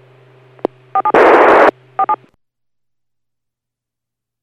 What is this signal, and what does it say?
DTMF, heard as telephone touch tones, tones of about 70 ms. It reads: 47 47